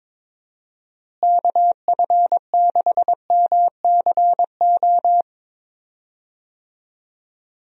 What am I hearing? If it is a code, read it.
Morse code, heard as a keyed tone, about 22 wpm, 700 Hz. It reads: KF6MCO
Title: USA Call Signs - 6 Characters - Repeated 3x - 22wpm